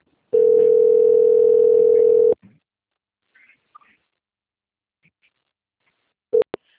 Ringback tone (North America), 440 + 480 Hz.